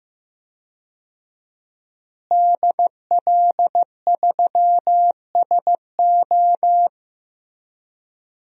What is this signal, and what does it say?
Morse code, heard as a keyed tone, about 15 wpm, 700 Hz. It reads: DL3SO
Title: DXCC-100 Call Signs - 5 Characters - Repeated 3x - 15wpm